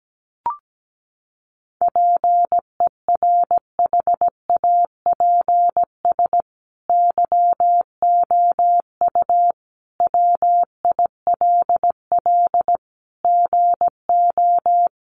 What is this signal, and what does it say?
Morse code, heard as a keyed tone, about 17 wpm, 700 Hz. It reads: PERHAPS YOU WILL GO
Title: Sentences from Top 500 Words 17wpm